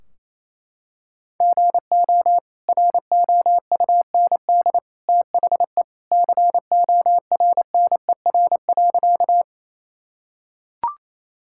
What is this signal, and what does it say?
Morse code, heard as a keyed tone, about 28 wpm, 700 Hz. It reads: GO ROUND THE CORNER.
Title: Sentences from Top 1600 Words 28wpm